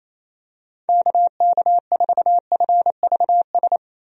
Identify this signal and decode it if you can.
Morse code, heard as a keyed tone, about 27 wpm, 700 Hz. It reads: KK4FVS
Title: USA Call Signs - 6 Characters - 28wpm